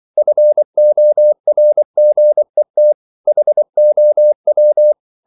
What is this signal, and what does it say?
Morse code, heard as a keyed tone, about 24 wpm, 600 Hz. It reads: FORGET HOW
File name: CW / Practice / EnglishQuotations0224